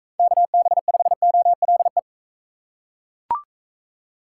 Morse code, heard as a keyed tone, about 38 wpm, 700 Hz. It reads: KB5OLE